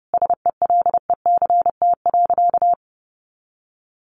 Morse code, heard as a keyed tone, about 30 wpm, 700 Hz. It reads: SELECT.
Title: Sentences from Top 1300 Words 30wpm